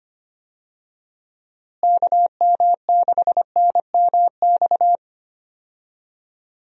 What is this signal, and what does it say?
Morse code, heard as a keyed tone, about 25 wpm, 700 Hz. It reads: KM6NMX
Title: USA Call Signs - 6 Characters - Repeated 2x - 25wpm